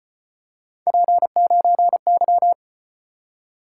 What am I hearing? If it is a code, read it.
Morse code, heard as a keyed tone, about 34 wpm, 700 Hz. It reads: P9Y